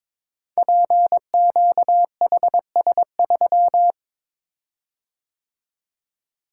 Morse code, heard as a keyed tone, about 22 wpm, 700 Hz. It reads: PQHS3
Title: Code Groups - 5 Letter:Numbers - 22wpm